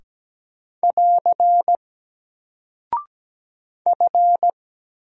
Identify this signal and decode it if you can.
Morse code, heard as a keyed tone, about 17 wpm, 700 Hz. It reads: <AR> F
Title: Single Character (Letter:Number:Punctuation:Prosign) 17wpm